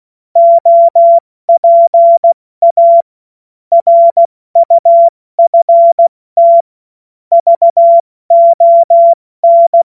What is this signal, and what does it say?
Morse code, heard as a keyed tone, about 16 wpm, 680 Hz. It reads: OPA RUFT VON
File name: Morsekurs / Witze / Witz 28. Dezember 2021, 70 F80